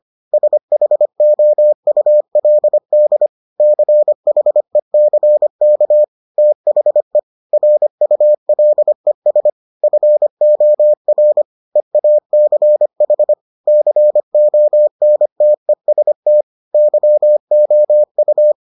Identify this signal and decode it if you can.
Morse code, heard as a keyed tone, about 25 wpm, 600 Hz. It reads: SHOULD CHECK THE RULES FOR EACH CONTEST YOU